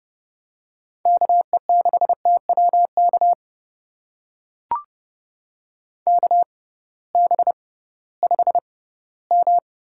Morse code, heard as a keyed tone, about 31 wpm, spaced as for 18 wpm, 700 Hz. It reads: KE6TWK KB5M
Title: USA Call Signs - 6 Characters - 6x Character Spacing - 30wpm